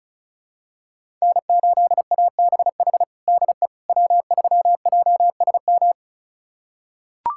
Morse code, heard as a keyed tone, about 34 wpm, 700 Hz. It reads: N8ABH DE W3JSM
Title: USA Callsign DE Callsign - 5 Characters - 35wpm